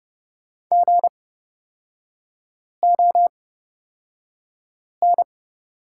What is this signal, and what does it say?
Morse code, heard as a keyed tone, about 29 wpm, spaced as for 5 wpm, 700 Hz. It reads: GON
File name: Lesson 36e - BT - Three Characters 30wpm